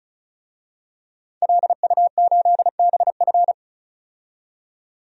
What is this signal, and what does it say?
Morse code, heard as a keyed tone, about 34 wpm, 700 Hz. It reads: LU8BF
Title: DXCC-100 Call Signs - 5 Characters - 35wpm